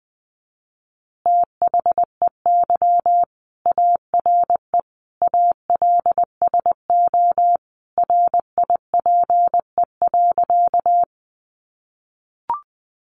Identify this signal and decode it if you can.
Morse code, heard as a keyed tone, about 19 wpm, 700 Hz. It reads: THEY ARE ALSO RIPE.
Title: Additional Easy Phrases 20wpm